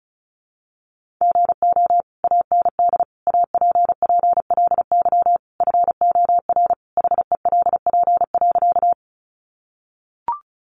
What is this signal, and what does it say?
Morse code, heard as a keyed tone, about 34 wpm, 700 Hz. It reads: GO AND APPLY FOR HELP.
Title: Sentences from Top 1200 Words 35wpm